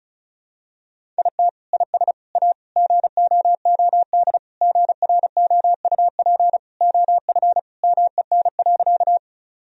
Morse code, heard as a keyed tone, about 34 wpm, 700 Hz. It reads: IT IS A GOOD GROUP OF MEN.
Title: Sentences from Top 1000 Words - Review Part 1 - 35wpm